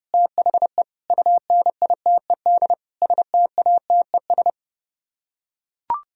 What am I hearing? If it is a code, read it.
Morse code, heard as a keyed tone, about 30 wpm, 700 Hz. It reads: THE UNITED STATES